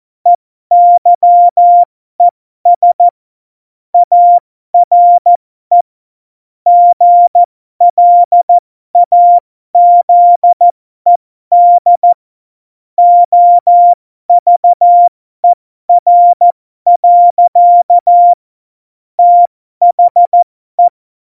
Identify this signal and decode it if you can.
Morse code, heard as a keyed tone, about 14 wpm, spaced as for 12 wpm, 700 Hz. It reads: EYES ARE GLAZED OVER. THE
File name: Morsekurs / Witze